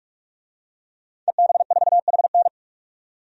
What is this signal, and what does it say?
Morse code, heard as a keyed tone, about 46 wpm, 700 Hz. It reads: EB4HN
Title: DXCC-100 Call Signs - 5 Characters - 45wpm